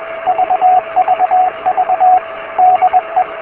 Morse code, heard as a keyed tone, about 21 wpm, 740 Hz, with radio noise. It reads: VVV DE
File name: beacons / k8uk